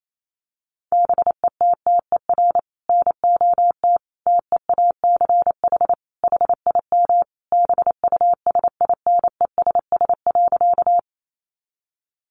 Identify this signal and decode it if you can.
Morse code, heard as a keyed tone, about 28 wpm, 700 Hz. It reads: BETTER NOT TEACH HIM BUSINESS.
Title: Sentences from Top 600 Words 28wpm